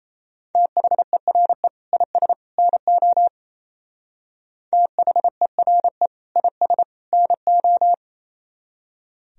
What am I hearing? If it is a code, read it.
Morse code, heard as a keyed tone, about 34 wpm, 700 Hz. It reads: THERE IS NO THERE IS NO
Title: Speed-Racing - Sets of 3 Words 28wpm